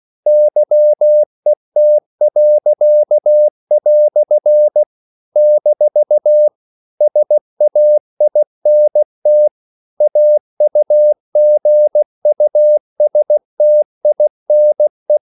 Morse code, heard as a keyed tone, about 16 wpm, 600 Hz. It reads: YET." - SAINT AUGUSTINE
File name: CW / Practice / EnglishQuotations0116